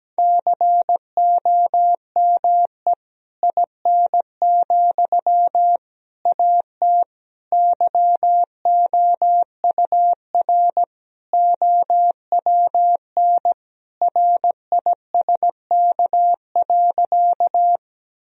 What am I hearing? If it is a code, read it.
Morse code, heard as a keyed tone, about 17 wpm, 700 Hz. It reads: COME IN, AT YOUR OWN RISK.
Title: Sentences from Top 1000 Words - Review Part 4 - 17wpm